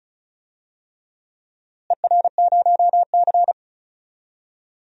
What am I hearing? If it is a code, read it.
Morse code, heard as a keyed tone, about 34 wpm, 700 Hz. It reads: ER0C
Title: DXCC-100 Call Signs - 4 Characters - Repeated 3x - 35wpm